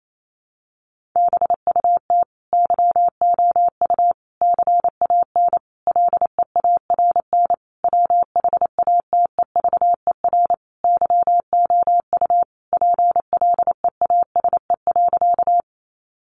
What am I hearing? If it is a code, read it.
Morse code, heard as a keyed tone, about 28 wpm, 700 Hz. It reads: BUT YOU CAN LEARN WHATEVER YOU PLEASE.
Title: Sentences from Top 1500 Words 28wpm